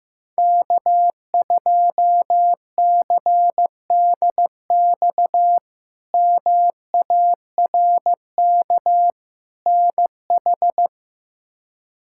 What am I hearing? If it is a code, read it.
Morse code, heard as a keyed tone, about 15 wpm, 700 Hz. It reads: K2CDX MARK NH
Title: K1USN SST QSO Practice - 15wpm